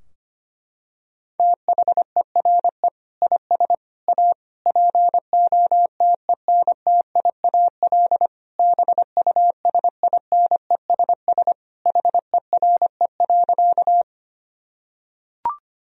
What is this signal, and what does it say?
Morse code, heard as a keyed tone, about 25 wpm, 700 Hz. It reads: THERE IS A POTENTIAL BUSINESS HERE.
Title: Sentences from Top 1000 Words - Review Part 4 - 25wpm